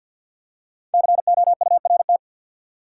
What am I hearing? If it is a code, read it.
Morse code, heard as a keyed tone, about 48 wpm, 700 Hz. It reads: COURT
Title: Five Letter Words - 50wpm